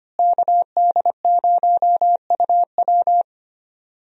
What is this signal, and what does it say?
Morse code, heard as a keyed tone, about 25 wpm, 700 Hz. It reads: KD0UW
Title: USA Call Signs - 5 Characters - Repeated 2x - 25wpm